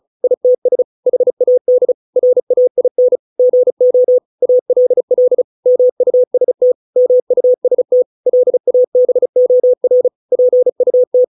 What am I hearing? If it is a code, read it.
Morse code, heard as a keyed tone, about 34 wpm, 500 Hz. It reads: ITS HAD RAIN GO ALL MUST MUST LABOR PUT